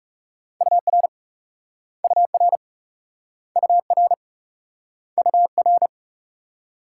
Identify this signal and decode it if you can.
Morse code, heard as a keyed tone, about 43 wpm, 700 Hz. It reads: UR UR UR UR